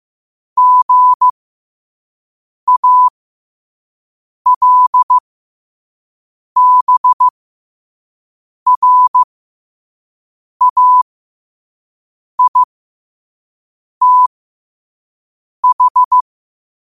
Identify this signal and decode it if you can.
Morse code, heard as a keyed tone, about 15 wpm, spaced as for 5 wpm, 1000 Hz. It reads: GALBRAITH
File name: qotdfiles / 5wpm